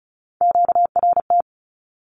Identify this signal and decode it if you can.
Morse code, heard as a keyed tone, about 34 wpm, 700 Hz. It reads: QRT
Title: Q-Codes 35wpm